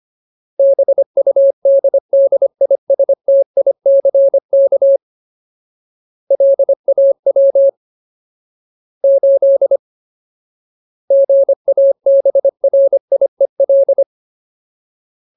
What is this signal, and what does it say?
Morse code, heard as a keyed tone, about 25 wpm, 550 Hz. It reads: BUDDISTICK LAW 8 GABRIEL
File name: Ultimate Word List / 25WPM Ultimate